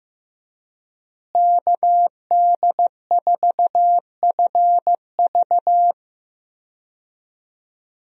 Morse code, heard as a keyed tone, about 15 wpm, 700 Hz. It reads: KD4FV
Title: USA Call Signs - 5 Characters - Repeated 2x - 15wpm